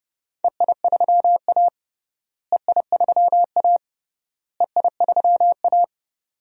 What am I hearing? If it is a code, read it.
Morse code, heard as a keyed tone, about 30 wpm, 700 Hz. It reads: EI3A EI3A EI3A
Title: DXCC-100 Call Signs - 4 Characters - Repeated 3x - 30wpm